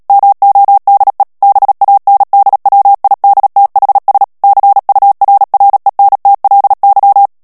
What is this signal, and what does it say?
Morse code, heard as a keyed tone, about 37 wpm, 800 Hz. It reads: MODE BANDWIDTHS CURRENTLY
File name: morse mp3 / ANGLAIS texte clair 3 vitesse 35 mots minute